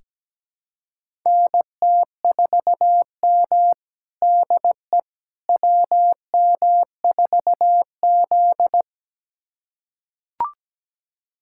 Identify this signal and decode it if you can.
Morse code, heard as a keyed tone, about 17 wpm, 700 Hz. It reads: NT4M DE WM4Z
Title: USA Callsign DE Callsign - 4 Characters - 17wpm